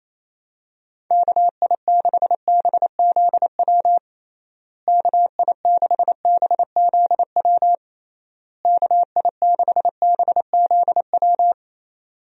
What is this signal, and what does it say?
Morse code, heard as a keyed tone, about 28 wpm, 700 Hz. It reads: KI6BZW KI6BZW KI6BZW
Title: USA Call Signs - 6 Characters - Repeated 3x - 28wpm